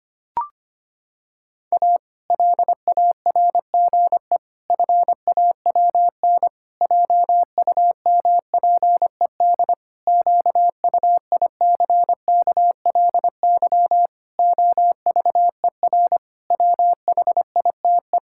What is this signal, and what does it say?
Morse code, heard as a keyed tone, about 25 wpm, 700 Hz. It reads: A LARGE FAWN JUMPED QUICKLY OVER WHITE
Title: Pangram Challenge v2 - 25wpm